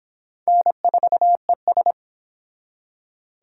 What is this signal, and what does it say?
Morse code, heard as a keyed tone, about 26 wpm, 700 Hz. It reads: N4ES
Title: Speed Racing - USA Call Signs 22wpm